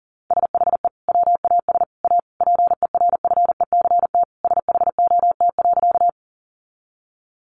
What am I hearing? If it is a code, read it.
Morse code, heard as a keyed tone, about 40 wpm, 700 Hz. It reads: SHE WAS A PERFECT SHOT.
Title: Sentences from Top 1100 Words 40wpm